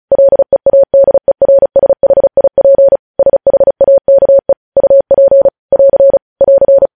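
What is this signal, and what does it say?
Morse code, heard as a keyed tone, about 36 wpm, 550 Hz. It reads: LEADERSHIP SHAKE UP <AR> <AR>
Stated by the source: Morse code news headlines sent at 35.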